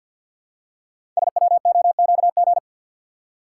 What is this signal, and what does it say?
Morse code, heard as a keyed tone, about 48 wpm, 700 Hz. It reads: SWOQG
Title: Code Groups - 5 Letters - 50wpm